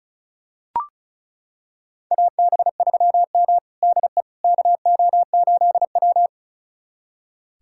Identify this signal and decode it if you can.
Morse code, heard as a keyed tone, about 36 wpm, 700 Hz. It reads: AB3M DE KO8W
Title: USA Callsign DE Callsign - 4 Characters - 35wpm